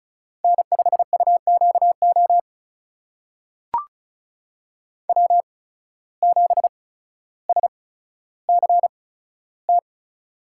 Morse code, heard as a keyed tone, about 36 wpm, spaced as for 15 wpm, 700 Hz. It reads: N5UQO W7SCT
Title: USA Call Signs - 8x Character Spacing - 35wpm